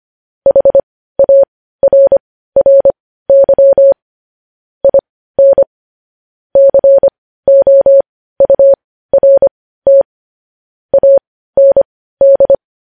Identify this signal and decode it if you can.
Morse code, heard as a keyed tone, about 25 wpm, spaced as for 15 wpm, 550 Hz. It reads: HARRY IN COURT AND
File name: podcast / News headlines-1x-25x15